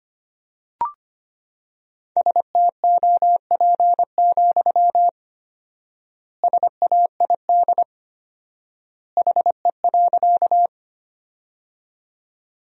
Morse code, heard as a keyed tone, about 25 wpm, 700 Hz. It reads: STOP, SAID HE.
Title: 4x Word-Spacing - Sentences from Top 300 Words - 25wpm